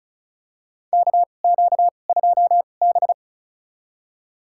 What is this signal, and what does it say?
Morse code, heard as a keyed tone, about 36 wpm, spaced as for 26 wpm, 690 Hz. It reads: KQ2B
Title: USA Call Signs - 4 Characters - 2x Character Spacing - 35wpm